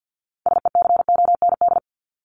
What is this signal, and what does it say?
Morse code, heard as a keyed tone, about 48 wpm, 700 Hz. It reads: SECOND
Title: Top 300 Words 50wpm